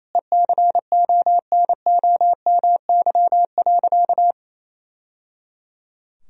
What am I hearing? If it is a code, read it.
Morse code, heard as a keyed tone, about 28 wpm, 700 Hz. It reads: ECONOMY.
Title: Sentences from Top 1000 Words - Review Part 3 - 28wpm